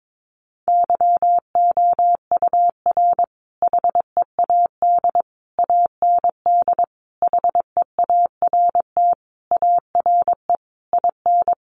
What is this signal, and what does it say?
Morse code, heard as a keyed tone, about 22 wpm, 700 Hz. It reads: YOUR HEAD AND HEART ARE IN
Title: Sentences from Top 1300 Words 22wpm